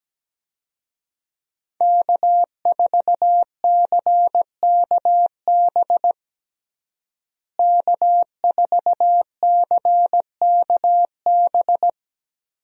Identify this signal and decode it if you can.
Morse code, heard as a keyed tone, about 17 wpm, 700 Hz. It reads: K4CKB K4CKB
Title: USA Call Signs - 5 Characters - Repeated 3x - 17wpm